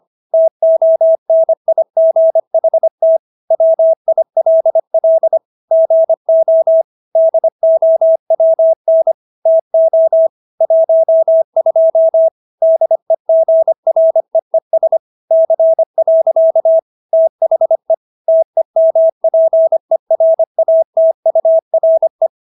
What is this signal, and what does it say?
Morse code, heard as a keyed tone, about 25 wpm, 650 Hz. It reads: TONIGHT WILL GO DOWN TO 12 DEGREES C. THE TEMPERATURE